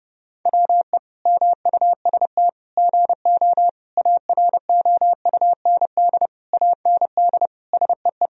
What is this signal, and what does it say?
Morse code, heard as a keyed tone, about 30 wpm, 700 Hz. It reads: WE MUST GO AROUND AND SEE
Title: Sentences from Top 1000 Words - Review Part 3 - 30wpm